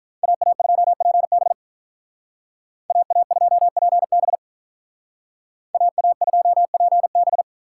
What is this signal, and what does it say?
Morse code, heard as a keyed tone, about 45 wpm, 700 Hz. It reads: AA2PB AA2PB AA2PB